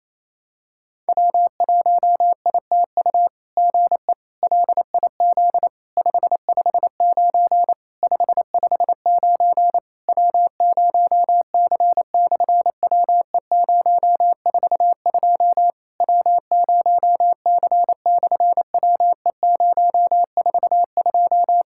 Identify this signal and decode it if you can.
Morse code, heard as a keyed tone, about 28 wpm, 700 Hz. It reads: W1ITU GE LIZ 559 559 W0C/WE042 W0C/WE042
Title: SOTA QSO Practice 28wpm